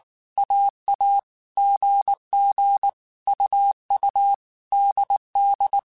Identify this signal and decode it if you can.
Morse code, heard as a keyed tone, about 19 wpm, 800 Hz. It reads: AA GG UU DD
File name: Groups@18 / Groups@18 mp3 / AWUNDG@18